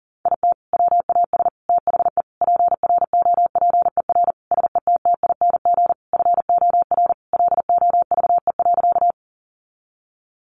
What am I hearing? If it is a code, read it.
Morse code, heard as a keyed tone, about 37 wpm, 700 Hz. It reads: IT WAS THE PROPER SETTING FOR LOVE.
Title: Sentences from Top 1300 Words 40wpm